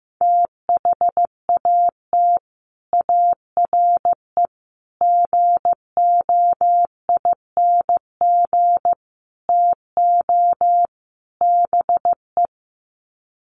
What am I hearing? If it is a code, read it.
Morse code, heard as a keyed tone, about 15 wpm, 700 Hz. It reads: THAT ARE GOING TO BE
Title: Sets of 5 Words Encore2 15wpm